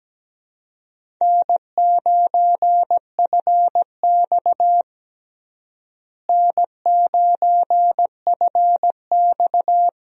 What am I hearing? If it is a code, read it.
Morse code, heard as a keyed tone, about 17 wpm, 700 Hz. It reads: N9FX N9FX
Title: USA Call Signs - 4 Characters - Repeated 2x - 17wpm